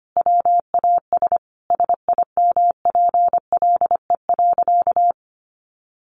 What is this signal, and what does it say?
Morse code, heard as a keyed tone, about 25 wpm, 700 Hz. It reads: WAS SIMPLE.
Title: Sentences from Top 1200 Words 25wpm